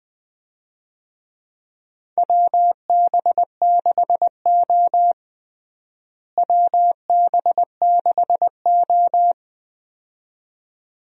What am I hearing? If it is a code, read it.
Morse code, heard as a keyed tone, about 21 wpm, 700 Hz. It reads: WB6O WB6O
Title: USA Call Signs - 4 Characters - Repeated 2x - 20wpm